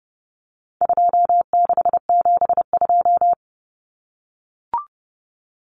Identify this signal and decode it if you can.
Morse code, heard as a keyed tone, about 30 wpm, 700 Hz. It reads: 2672
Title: Numbers - 4-Digit 30wpm